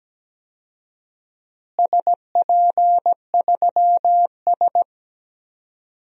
Morse code, heard as a keyed tone, about 17 wpm, 700 Hz. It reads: SP3S
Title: DXCC-100 Call Signs - 4 Characters - Repeated 2x - 17wpm